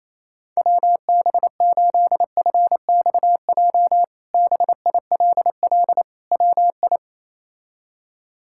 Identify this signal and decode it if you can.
Morse code, heard as a keyed tone, about 28 wpm, 700 Hz. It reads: WB8FXJ BILL WI
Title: K1USN SST QSO Practice - 28wpm